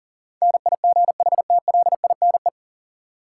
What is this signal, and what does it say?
Morse code, heard as a keyed tone, about 40 wpm, 700 Hz. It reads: NIGHTLINE
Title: Sets of 5 Words - New Word Drill - 40wpm